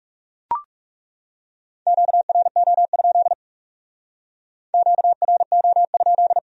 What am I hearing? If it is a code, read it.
Morse code, heard as a keyed tone, about 37 wpm, 700 Hz. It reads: QRO? QRO?
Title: Speed-Racing Q-Codes - 30wpm